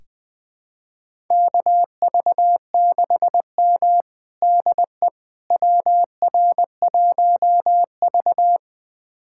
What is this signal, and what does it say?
Morse code, heard as a keyed tone, about 21 wpm, 700 Hz. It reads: KV6M DE WR1V
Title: USA Callsign DE Callsign - 4 Characters - 20wpm